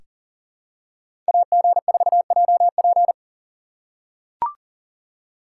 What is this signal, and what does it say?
Morse code, heard as a keyed tone, about 40 wpm, 700 Hz. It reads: AG4JP
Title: USA Call Signs - 5 Characters - 3x Character Spacing - 40wpm